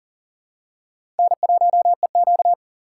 Morse code, heard as a keyed tone, about 40 wpm, 700 Hz. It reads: N1EQ
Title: USA Call Signs - 3x Character Spacing - 40wpm